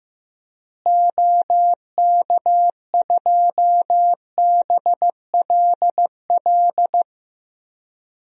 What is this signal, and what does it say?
Morse code, heard as a keyed tone, about 15 wpm, 700 Hz. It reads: OK2BLL
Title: DXCC-100 Call Signs - 6 Characters - Repeated 3x - 15wpm